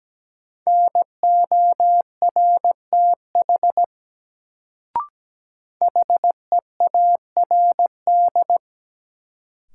Morse code, heard as a keyed tone, about 17 wpm, 700 Hz. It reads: NORTH HEARD
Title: Sentences from Top 400 Words - New Word Drill - 17wpm